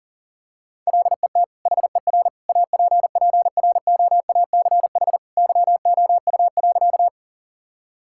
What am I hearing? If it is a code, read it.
Morse code, heard as a keyed tone, about 40 wpm, 700 Hz. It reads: LET HER APPROACH YOU.
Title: Sentences from Top 1000 Words - Review Part 6 - 40wpm